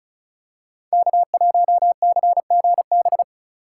Morse code, heard as a keyed tone, about 36 wpm, 690 Hz. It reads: K1CGB
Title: Code Groups - 5 Letter:Numbers - 35wpm